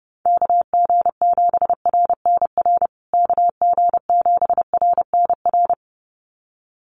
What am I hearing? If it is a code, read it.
Morse code, heard as a keyed tone, about 30 wpm, 700 Hz. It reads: KG7RNR KG7RNR
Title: USA Call Signs Repeated 2x 30wpm